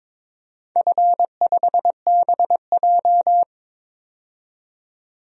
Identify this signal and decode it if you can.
Morse code, heard as a keyed tone, about 22 wpm, 700 Hz. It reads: F5BJ
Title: DXCC-100 Call Signs - 4 Characters - 22wpm